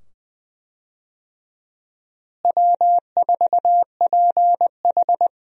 Morse code, heard as a keyed tone, about 20 wpm, 700 Hz. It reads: W4PH
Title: USA Call Signs - 4 Characters - Repeated 2x - 20wpm